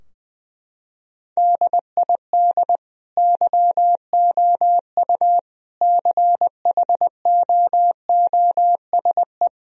Morse code, heard as a keyed tone, about 20 wpm, 700 Hz. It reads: DID YOU CHOOSE
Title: Sentences from Top 1000 Words - Review Part 4 - 20wpm